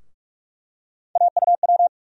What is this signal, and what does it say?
Morse code, heard as a keyed tone, about 46 wpm, 700 Hz. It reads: AUW
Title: USA Call Sign Postfixes - 3 Letters 45wpm